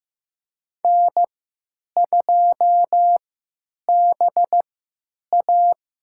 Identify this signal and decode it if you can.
Morse code, heard as a keyed tone, about 15 wpm, spaced as for 9 wpm, 700 Hz. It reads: N2BA
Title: USA Call Signs - 3x Character Spacing - 15wpm